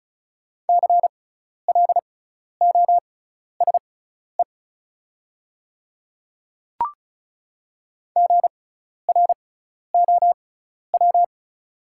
Morse code, heard as a keyed tone, about 36 wpm, spaced as for 12 wpm, 700 Hz. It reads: CLOSE GROW
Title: 6x Character Spacing - Top 500 Words - 35wpm